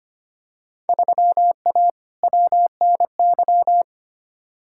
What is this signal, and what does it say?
Morse code, heard as a keyed tone, about 25 wpm, 700 Hz. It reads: 3A WNY
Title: Field Day Exchanges - 25wpm